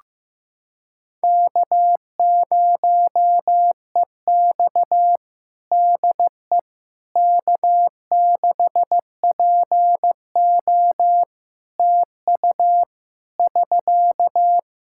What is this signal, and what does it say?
Morse code, heard as a keyed tone, about 14 wpm, 700 Hz. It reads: K0EX DE K6PO TU <SK>
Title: Practice QSO - Exchange 4 - Ending - 15wpm